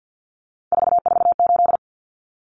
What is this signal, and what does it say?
Morse code, heard as a keyed tone, about 50 wpm, 700 Hz. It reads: 448
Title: Common Cut Numbers - 3-Digit 50wpm